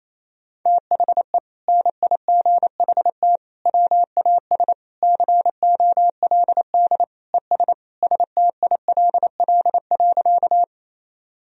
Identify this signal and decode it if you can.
Morse code, heard as a keyed tone, about 28 wpm, 700 Hz. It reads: THE NIGHT WAS COLD ES STILL.
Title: Sentences from Top 400 Words and Abbreviations 28wpm